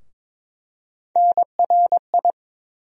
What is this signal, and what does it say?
Morse code, heard as a keyed tone, about 22 wpm, 700 Hz. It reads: NRI
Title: USA Call Sign Postfixes - 3 Letters 22wpm